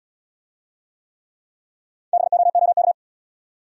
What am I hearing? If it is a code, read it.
Morse code, heard as a keyed tone, about 68 wpm, 700 Hz. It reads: BOOK